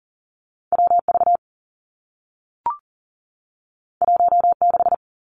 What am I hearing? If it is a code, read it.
Morse code, heard as a keyed tone, about 41 wpm, 700 Hz. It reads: WV 16
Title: Two Challenging Letter-Number Combinations 40wpm